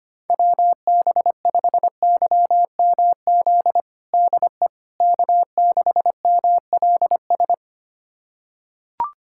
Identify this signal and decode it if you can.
Morse code, heard as a keyed tone, about 25 wpm, 700 Hz. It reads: WB5YMZ DE K6MLS
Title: USA Callsign DE Callsign 25wpm